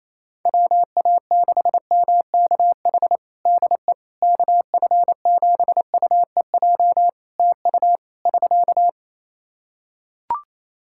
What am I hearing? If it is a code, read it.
Morse code, heard as a keyed tone, about 28 wpm, 700 Hz. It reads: WA6MKH DE KF7UEJ TU <SK>
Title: K1USN SST QSO Practice 28wpm